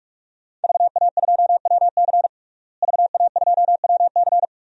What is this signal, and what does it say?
Morse code, heard as a keyed tone, about 47 wpm, 700 Hz. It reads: VA2WC VA2WC
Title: DXCC-100 Call Signs - 5 Characters - Repeated 3x - 45wpm